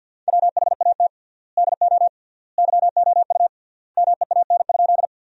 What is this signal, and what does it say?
Morse code, heard as a keyed tone, about 50 wpm, 700 Hz. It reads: WHAT DO YOU MEAN?
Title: 3x Word-Spacing - Sentences from Top 200 Words - 50wpm